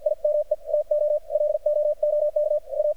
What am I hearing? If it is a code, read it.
Morse code, heard as a keyed tone, about 58 wpm, spaced as for 47 wpm, 600 Hz. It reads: IME TO LOOK A
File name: archived / CRSnet